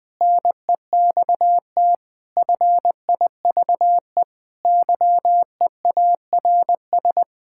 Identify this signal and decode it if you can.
Morse code, heard as a keyed tone, about 20 wpm, 700 Hz. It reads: NEXT FIVE YEARS
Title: Build A Phrase - 5-Word Phrases - 20wpm